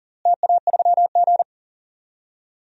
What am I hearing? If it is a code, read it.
Morse code, heard as a keyed tone, about 39 wpm, 700 Hz. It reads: TA3G
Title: DXCC-100 Call Signs - 4 Characters - Repeated 3x - 40wpm